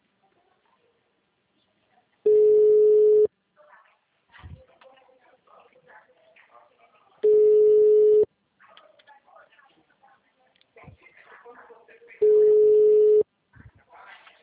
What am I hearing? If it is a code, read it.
Ringback tone (Europe, 425 Hz), with speech.
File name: out- -2017- - - .545